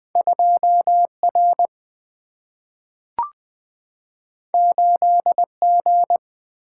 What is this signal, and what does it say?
Morse code, heard as a keyed tone, about 20 wpm, 700 Hz. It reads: 2 R 8 G
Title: Single-Letter Single-Number 20wpm